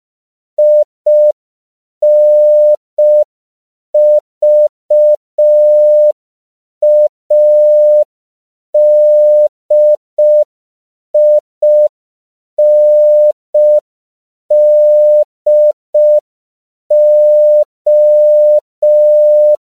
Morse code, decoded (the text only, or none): INVADINDO